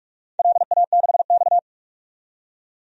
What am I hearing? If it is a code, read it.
Morse code, heard as a keyed tone, about 44 wpm, 700 Hz. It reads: LA6X
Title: DXCC-100 Call Signs - 4 Characters - 45wpm